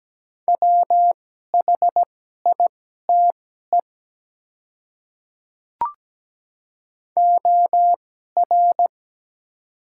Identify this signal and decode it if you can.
Morse code, heard as a keyed tone, about 17 wpm, spaced as for 12 wpm, 700 Hz. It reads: WHITE OR
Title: 2x Character Spacing - Top 300 Words - 17wpm